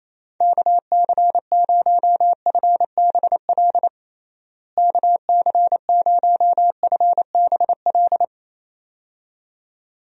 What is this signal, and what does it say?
Morse code, heard as a keyed tone, about 27 wpm, 700 Hz. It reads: KC0FBL KC0FBL
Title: USA Call Signs - 6 Characters - Repeated 2x - 28wpm